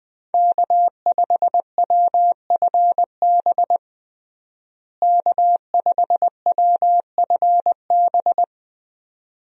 Morse code, heard as a keyed tone, about 20 wpm, 700 Hz. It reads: K5WFB K5WFB
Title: USA Call Signs - 5 Characters - Repeated 2x - 20wpm